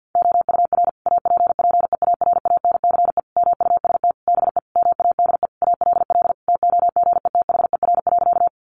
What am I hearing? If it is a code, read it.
Morse code, heard as a keyed tone, about 50 wpm, 700 Hz. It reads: OUR APPEARANCE MUST BE MADE ALL TOGETHER.
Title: Sentences from Top 1100 Words 50wpm